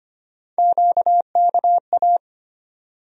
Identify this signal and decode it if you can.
Morse code, heard as a keyed tone, about 25 wpm, 700 Hz. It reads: QKA
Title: USA Call Sign Postfixes - 3 Letters 25wpm